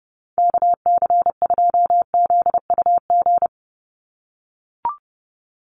Morse code, heard as a keyed tone, about 30 wpm, 700 Hz. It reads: KC2ZUG
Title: USA Call Signs 30wpm